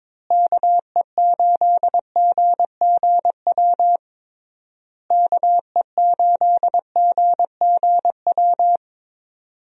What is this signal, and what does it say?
Morse code, heard as a keyed tone, about 22 wpm, 700 Hz. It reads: KE8GGW KE8GGW
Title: USA Call Signs - 6 Characters - Repeated 3x - 22wpm